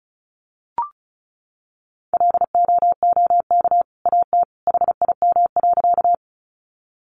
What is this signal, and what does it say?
Morse code, heard as a keyed tone, about 34 wpm, 700 Hz. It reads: LOOK AT HIM.
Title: Sentences from Top 100 Words 35wpm